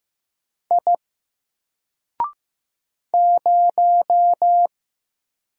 Morse code, heard as a keyed tone, about 15 wpm, 700 Hz. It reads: I 0
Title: Single Character (Letter:Number:Punctuation:Prosign) Alt 15wpm